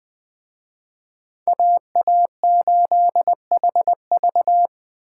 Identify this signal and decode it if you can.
Morse code, heard as a keyed tone, about 21 wpm, 700 Hz. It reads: AA8HV